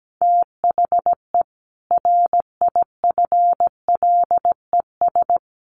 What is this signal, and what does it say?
Morse code, heard as a keyed tone, about 17 wpm, 700 Hz. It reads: THE RIFLES
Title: A Sound Of Thunder 17wpm